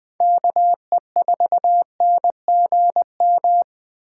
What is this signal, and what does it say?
Morse code, heard as a keyed tone, about 20 wpm, 700 Hz. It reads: KE4NGM
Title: USA Call Signs - 6 Characters - 8x Character Spacing - 20wpm